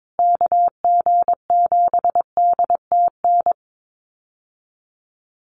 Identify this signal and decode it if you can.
Morse code, heard as a keyed tone, about 22 wpm, 700 Hz. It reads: KG7DTN
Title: USA Call Signs Repeated 2x 22wpm